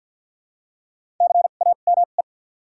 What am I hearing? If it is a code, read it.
Morse code, heard as a keyed tone, about 53 wpm, spaced as for 37 wpm, 700 Hz. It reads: CAME
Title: 2x Character Spacing - Top 500 Words - 50wpm